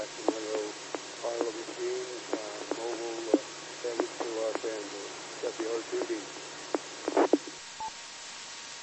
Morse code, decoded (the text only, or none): none